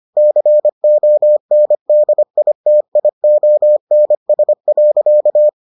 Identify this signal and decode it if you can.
Morse code, heard as a keyed tone, about 25 wpm, 600 Hz. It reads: CONDITIONS.